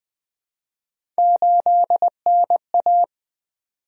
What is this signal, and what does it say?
Morse code, heard as a keyed tone, about 20 wpm, 700 Hz. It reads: 8NA
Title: Three Letter-Number Combinations - 20wpm